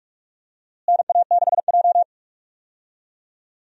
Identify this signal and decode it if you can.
Morse code, heard as a keyed tone, about 43 wpm, 700 Hz. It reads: NA6J
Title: USA Call Signs - 4 Characters - Repeated 3x - 45wpm